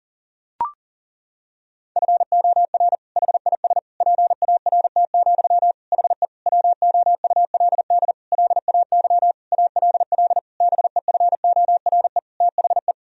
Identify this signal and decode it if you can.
Morse code, heard as a keyed tone, about 40 wpm, 700 Hz. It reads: FOR HIS PART, HE WOULD LAY ALL BEFORE THE
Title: Sentences from Top 1000 Words - Review Part 2 - 40wpm